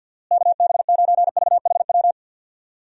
Morse code, heard as a keyed tone, about 51 wpm, 700 Hz. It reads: KB9VHW
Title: USA Call Signs - 6 Characters - Repeated 2x - 50wpm